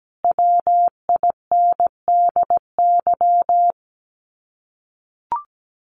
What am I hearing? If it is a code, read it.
Morse code, heard as a keyed tone, about 17 wpm, 700 Hz. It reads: WINDY
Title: Common QSO Phrases Encore 17wpm